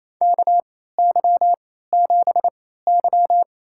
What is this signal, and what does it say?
Morse code, heard as a keyed tone, about 28 wpm, spaced as for 16 wpm, 700 Hz. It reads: KY7Y
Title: USA Call Signs - 3x Character Spacing - 28wpm